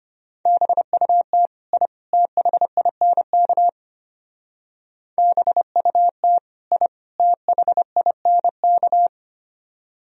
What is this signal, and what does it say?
Morse code, heard as a keyed tone, about 31 wpm, 700 Hz. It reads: BUT I THINK BUT I THINK
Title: Speed-Racing - Sets of 3 Words 25wpm